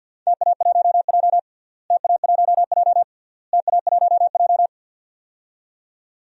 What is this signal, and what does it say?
Morse code, heard as a keyed tone, about 49 wpm, 700 Hz. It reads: TA1J TA1J TA1J
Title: DXCC-100 Call Signs - 4 Characters - Repeated 3x - 50wpm